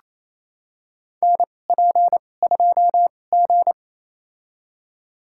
Morse code, heard as a keyed tone, about 28 wpm, spaced as for 20 wpm, 700 Hz. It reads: NP2G
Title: USA Call Signs - 2x Character Spacing - 28wpm